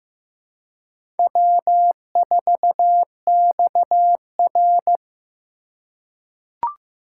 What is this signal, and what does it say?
Morse code, heard as a keyed tone, about 15 wpm, 700 Hz. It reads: W4XR
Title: USA Call Signs - 4x Character Spacing - 15wpm